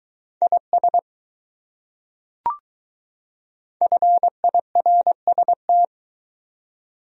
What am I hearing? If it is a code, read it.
Morse code, heard as a keyed tone, about 20 wpm, 700 Hz. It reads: IS FIRST